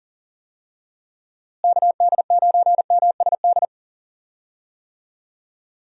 Morse code, heard as a keyed tone, about 40 wpm, 700 Hz. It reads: KD9MSD
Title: USA Call Signs - 6 Characters - Repeated 2x - 40wpm